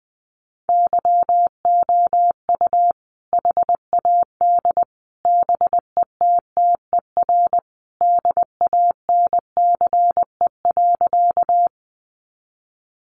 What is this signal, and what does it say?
Morse code, heard as a keyed tone, about 20 wpm, 700 Hz. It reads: YOU HAD BETTER DANCE.
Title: Sentences from Top 700 Words 20wpm